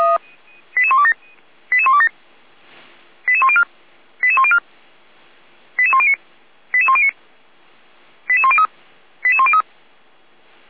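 Five-tone selective call (ZVEI), heard as quick runs of five tones, tones of about 70 ms. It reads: 88127 88127 88104 88104 88109 88109 88103 88103
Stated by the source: ZVEI five-tone selective call 88402 received by SDR, Feuerwehr Orlamünde.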